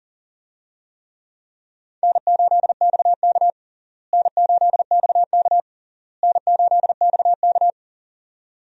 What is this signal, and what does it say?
Morse code, heard as a keyed tone, about 40 wpm, 700 Hz. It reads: N8XK N8XK N8XK
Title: USA Call Signs - 4 Characters - Repeated 3x - 40wpm